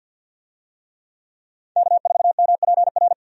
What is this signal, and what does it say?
Morse code, heard as a keyed tone, about 50 wpm, 700 Hz. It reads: K4MPR
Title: USA Call Signs - 5 Characters - 50wpm